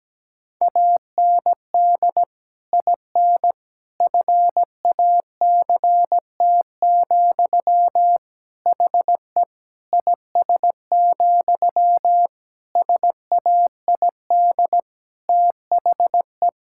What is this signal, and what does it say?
Morse code, heard as a keyed tone, about 17 wpm, 700 Hz. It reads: AND IN FACT, HE IS, SAID THE
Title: Sentences from Top 1600 Words 17wpm